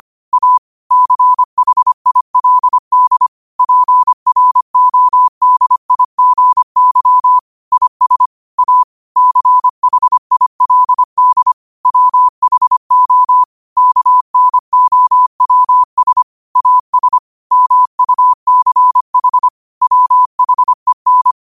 Morse code, heard as a keyed tone, about 25 wpm, 1000 Hz. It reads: A CHILD PRODIGY IS A CHILD WHO KNOWS AS MUCH WHEN
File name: 25wpm